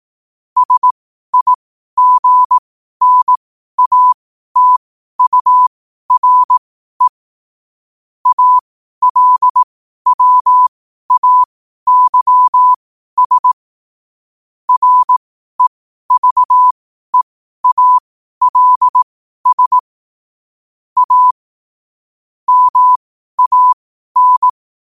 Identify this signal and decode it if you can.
Morse code, heard as a keyed tone, about 18 wpm, spaced as for 12 wpm, 1000 Hz. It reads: SIGNATURE ALWAYS REVEALS A MAN
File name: qotdfiles / 12wpm